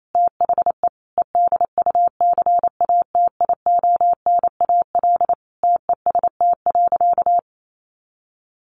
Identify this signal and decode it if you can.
Morse code, heard as a keyed tone, about 28 wpm, 700 Hz. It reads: THE EDUCATIONAL TEST.
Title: Sentences from Top 1200 Words 28wpm